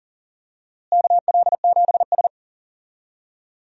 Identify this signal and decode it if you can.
Morse code, heard as a keyed tone, about 39 wpm, 700 Hz. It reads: KL7S
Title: DXCC-100 Call Signs - 4 Characters - Repeated 3x - 40wpm